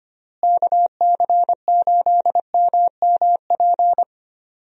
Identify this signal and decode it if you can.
Morse code, heard as a keyed tone, about 25 wpm, 700 Hz. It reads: KC8MMP
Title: USA Call Signs - 6 Characters - Repeated 3x - 25wpm